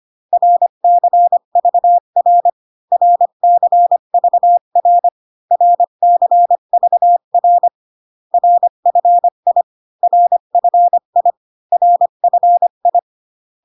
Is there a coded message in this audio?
Yes: Morse code from 0.3 to 13.0 s, heard as a keyed tone, about 25 wpm, 700 Hz. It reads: RCVR RCVR RCVR RFI RFI RFI